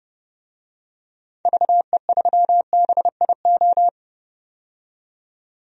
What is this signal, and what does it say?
Morse code, heard as a keyed tone, about 30 wpm, 700 Hz. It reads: VE3BIO
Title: DXCC-100 Call Signs - 6 Characters - 30wpm